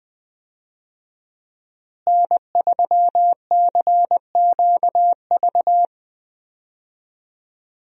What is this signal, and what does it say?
Morse code, heard as a keyed tone, about 20 wpm, 700 Hz. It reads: N3CQV
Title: USA Call Signs - 5 Characters - Repeated 2x - 20wpm